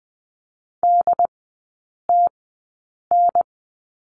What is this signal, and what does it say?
Morse code, heard as a keyed tone, about 20 wpm, spaced as for 8 wpm, 700 Hz. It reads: DTN
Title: 067 Lesson 12e - Letter d - Three Characters 20wpm